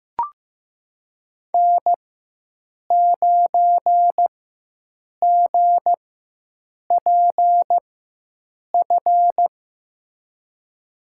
Morse code, heard as a keyed tone, about 15 wpm, spaced as for 7 wpm, 700 Hz. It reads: N9GPF
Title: USA Call Signs - 5 Characters - 4x Character Spacing - 15wpm